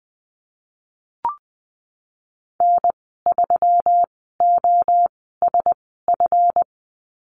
Morse code, heard as a keyed tone, about 20 wpm, spaced as for 15 wpm, 700 Hz. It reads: N3OSF
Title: 173 Lesson 28g - Number 9 - Call Signs 20wpm